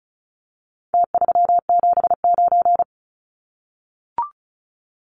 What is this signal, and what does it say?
Morse code, heard as a keyed tone, about 34 wpm, 700 Hz. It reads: T379
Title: Numbers - Contest Style - 4-Digit 35wpm